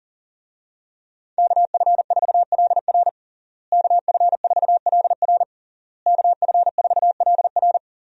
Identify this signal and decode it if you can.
Morse code, heard as a keyed tone, about 37 wpm, 700 Hz. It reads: KF4LR KF4LR KF4LR
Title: USA Call Signs - 5 Characters - Repeated 3x - 40wpm